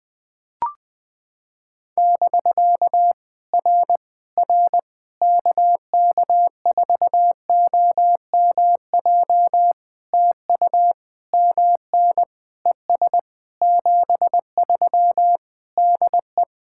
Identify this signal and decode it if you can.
Morse code, heard as a keyed tone, about 19 wpm, 700 Hz. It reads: <BK> R R KK4OMJ TU MN ES 73 DE
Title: POTA QSO Practice - 20wpm